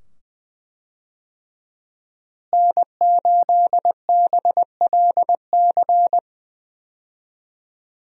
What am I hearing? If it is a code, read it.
Morse code, heard as a keyed tone, about 20 wpm, 700 Hz. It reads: N8BLC
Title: USA Call Signs - 5 Characters - Repeated 2x - 20wpm